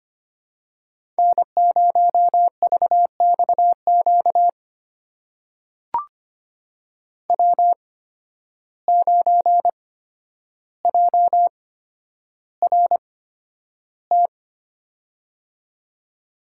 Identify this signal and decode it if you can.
Morse code, heard as a keyed tone, about 25 wpm, spaced as for 11 wpm, 700 Hz. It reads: N0VXQ W9JRT
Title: USA Call Signs - 5 Characters - 8x Character Spacing - 25wpm